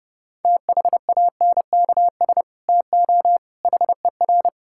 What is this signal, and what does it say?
Morse code, heard as a keyed tone, about 30 wpm, 700 Hz. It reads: THANKS TO HER
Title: Sentences from Top 1000 Words - Review Part 5 - 30wpm